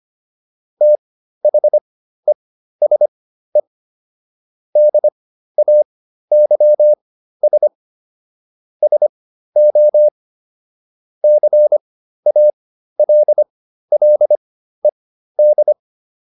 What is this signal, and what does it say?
Morse code, heard as a keyed tone, about 25 wpm, spaced as for 13 wpm, 600 Hz. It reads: THESE DAYS SO CALLED